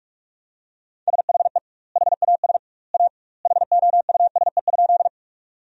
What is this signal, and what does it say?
Morse code, heard as a keyed tone, about 46 wpm, 700 Hz. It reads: SHE HAS A HOUSE?
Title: 2x Word-Spacing - Sentences from Top 200 Words - 45wpm